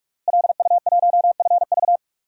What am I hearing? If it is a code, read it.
Morse code, heard as a keyed tone, about 46 wpm, 700 Hz. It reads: LU1FV
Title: DXCC-100 Call Signs - 5 Characters - Repeated 3x - 45wpm